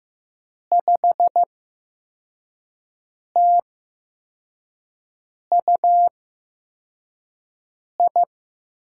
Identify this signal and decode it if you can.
Morse code, heard as a keyed tone, about 15 wpm, spaced as for 4 wpm, 700 Hz. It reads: 5TUI
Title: USA Call Signs - 5 Characters - 8x Character Spacing - 15wpm